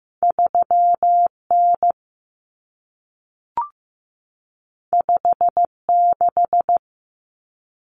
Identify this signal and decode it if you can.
Morse code, heard as a keyed tone, about 15 wpm, 700 Hz. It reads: 3N 56
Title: Common Cut Numbers - 2-Digit 15wpm